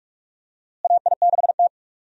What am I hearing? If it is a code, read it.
Morse code, heard as a keyed tone, about 46 wpm, 700 Hz. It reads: AI6T
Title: USA Call Signs - 4 Characters - Repeated 3x - 45wpm